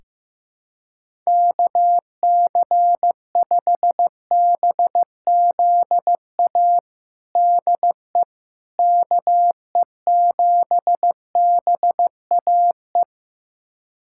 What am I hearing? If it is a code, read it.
Morse code, heard as a keyed tone, about 15 wpm, 700 Hz. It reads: KC5BZA DE KE7BAE
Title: USA Callsign DE Callsign 15wpm